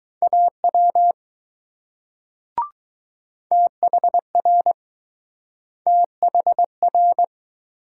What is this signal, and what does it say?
Morse code, heard as a keyed tone, about 21 wpm, 700 Hz. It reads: AW THR THR